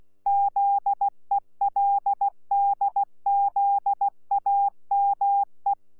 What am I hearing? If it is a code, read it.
Morse code, heard as a keyed tone, about 16 wpm, 800 Hz. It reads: ZELDZAME
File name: Z PI4VRZ / PI4VRZ 14-jun-2025